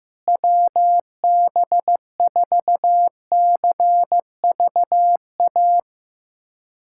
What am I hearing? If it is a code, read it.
Morse code, heard as a keyed tone, about 15 wpm, 700 Hz. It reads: WB4CVA
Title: USA Callsign DE Callsign 15wpm